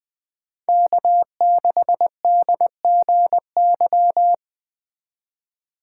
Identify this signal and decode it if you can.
Morse code, heard as a keyed tone, about 20 wpm, 700 Hz. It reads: K6DGY
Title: USA Call Signs - 2x Character Spacing - 20wpm